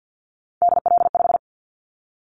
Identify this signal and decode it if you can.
Morse code, heard as a keyed tone, about 50 wpm, 700 Hz. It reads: DL5
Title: Callsign Prefixes for Top 25 DXCC - 50wpm